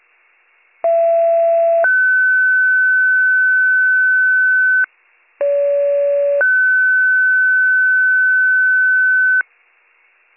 Two-tone page, heard as a one-second tone, then a three-second tone, about 670 then 1500 Hz, about 570 then 1500 Hz. It is heard from 0.8 to 9.4 s.